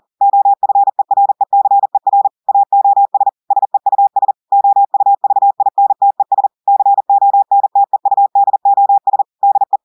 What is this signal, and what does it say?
Morse code, heard as a keyed tone, about 40 wpm, 800 Hz. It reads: OFERECER AOS SEUS OUVINTES CONTEUDOS DE
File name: audios / portugais / portugais texte en clair 2 vitesse 40 mots minutes